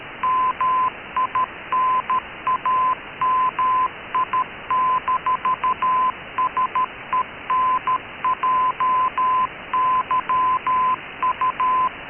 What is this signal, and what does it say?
Morse code, heard as a keyed tone, about 13 wpm, 1020 Hz. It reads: MINAMI-SENJYU
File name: KKS-1 cw 21jul2014 1425z